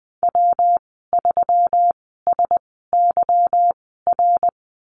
Morse code, heard as a keyed tone, about 20 wpm, spaced as for 15 wpm, 700 Hz. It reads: W3SYR
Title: 179 Lesson 29g - Slash - Call Signs 20wpm